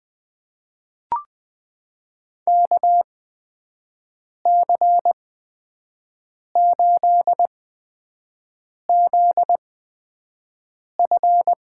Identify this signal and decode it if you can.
Morse code, heard as a keyed tone, about 20 wpm, spaced as for 5 wpm, 700 Hz. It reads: KC8ZF
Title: USA Call Signs - 8x Character Spacing - 20wpm